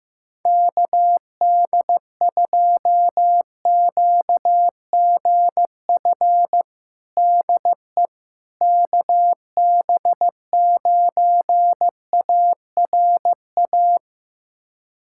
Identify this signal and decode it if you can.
Morse code, heard as a keyed tone, about 15 wpm, 700 Hz. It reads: KD2QGF DE KB9ARA
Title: USA Callsign DE Callsign - 6 Characters - 15wpm